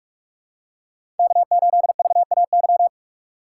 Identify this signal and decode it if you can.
Morse code, heard as a keyed tone, about 46 wpm, 700 Hz. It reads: K8VAY
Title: USA Call Signs - 5 Characters - 3x Character Spacing - 45wpm